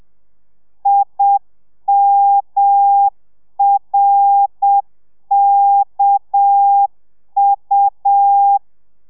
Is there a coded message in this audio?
Yes: Morse code from 0.9 to 8.6 s, heard as a keyed tone, about 7 wpm, 800 Hz. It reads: IMRKU